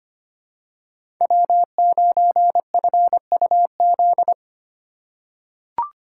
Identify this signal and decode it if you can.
Morse code, heard as a keyed tone, about 25 wpm, 700 Hz. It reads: W9FUZ
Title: USA Call Signs - 2x Character Spacing - 25wpm